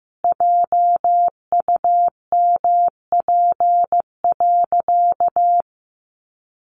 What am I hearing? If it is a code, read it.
Morse code, heard as a keyed tone, about 15 wpm, 700 Hz. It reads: JUMP.
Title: Sentences from Top 1300 Words 15wpm